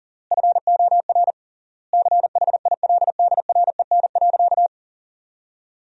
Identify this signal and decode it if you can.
Morse code, heard as a keyed tone, about 40 wpm, 700 Hz. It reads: FOR CHILDREN.
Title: 3x Word-Spacing - Sentences from Top 400 Words - 40wpm